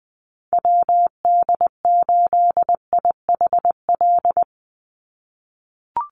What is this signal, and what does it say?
Morse code, heard as a keyed tone, about 20 wpm, 700 Hz. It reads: WD8IHL
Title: USA Call Signs 20wpm